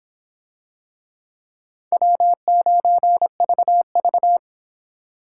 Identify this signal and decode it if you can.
Morse code, heard as a keyed tone, about 24 wpm, 700 Hz. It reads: W9VV